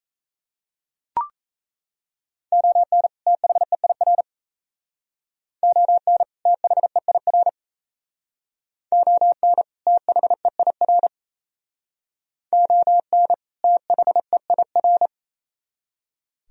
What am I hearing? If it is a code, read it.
Morse code, heard as a keyed tone, about 35 wpm, 700 Hz. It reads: ON THEIR ON THEIR ON THEIR ON THEIR
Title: Speed-Racing - Sets of 2 Words 28wpm